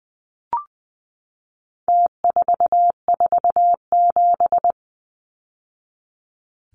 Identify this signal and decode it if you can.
Morse code, heard as a keyed tone, about 20 wpm, 700 Hz. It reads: T447
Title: Numbers - Contest Style - 4-Digit 20wpm